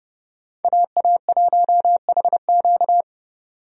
Morse code, heard as a keyed tone, about 32 wpm, 700 Hz. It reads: AA1HQ
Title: USA Call Signs - 5 Characters - 30wpm